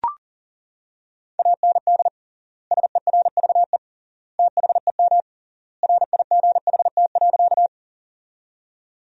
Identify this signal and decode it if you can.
Morse code, heard as a keyed tone, about 40 wpm, 700 Hz. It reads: AND SERVE THEM RIGHT.
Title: 3x Word-Spacing - Sentences from Top 400 Words - 40wpm